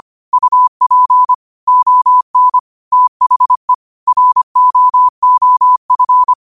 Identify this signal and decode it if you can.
Morse code, heard as a keyed tone, about 25 wpm, 1000 Hz. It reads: UP ON THE ROOF
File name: qotdfiles / 25wpm